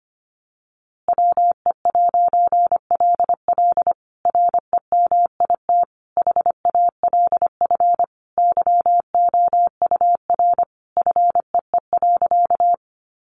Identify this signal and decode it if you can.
Morse code, heard as a keyed tone, about 25 wpm, 700 Hz. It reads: WE'LL REMIT HALF YOUR FEE.
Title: A Sound Of Thunder 25wpm